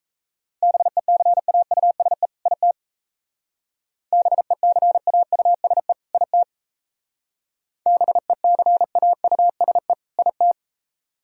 Morse code, heard as a keyed tone, about 34 wpm, 700 Hz. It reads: BECAUSE IT BECAUSE IT BECAUSE IT
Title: Speed-Racing - Sets of 2 Words 28wpm